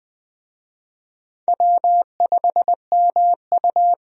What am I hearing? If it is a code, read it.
Morse code, heard as a keyed tone, about 21 wpm, 700 Hz. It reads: W5MU
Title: USA Call Signs - 4 Characters - 20wpm